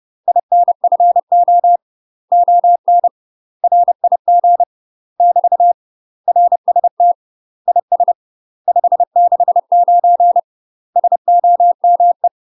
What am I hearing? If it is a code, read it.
Morse code, heard as a keyed tone, about 30 wpm, 700 Hz. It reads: INFO ON RIG <BT> RST IS 569 SOME
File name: UK MASTER QSOs TRAINING / 30WPM / UK MASTER QSOs 0003